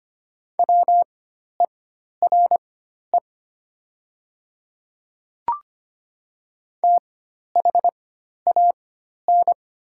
Morse code, heard as a keyed tone, about 25 wpm, spaced as for 12 wpm, 700 Hz. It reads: WERE THAN
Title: 4x Character Spacing - Top 100 Words - 25wpm